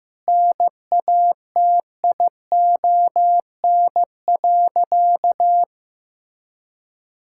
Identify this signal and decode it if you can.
Morse code, heard as a keyed tone, about 15 wpm, 700 Hz. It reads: NATION.
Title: Sentences from Top 1000 Words - Review Part 6 - 15wpm